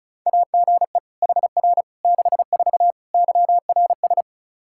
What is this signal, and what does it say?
Morse code, heard as a keyed tone, about 36 wpm, 700 Hz. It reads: AGE HR 64 YRS
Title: K1USN SST QSO Practice 35wpm